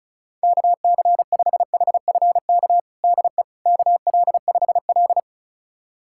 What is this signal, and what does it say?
Morse code, heard as a keyed tone, about 34 wpm, 700 Hz. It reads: KC5HFK DE KL5L
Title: USA Callsign DE Callsign 35wpm